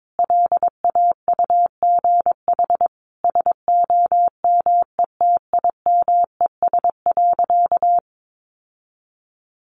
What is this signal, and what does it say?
Morse code, heard as a keyed tone, about 22 wpm, 700 Hz. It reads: LAUGH SOMETIMES.
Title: Sentences from Top 600 Words 22wpm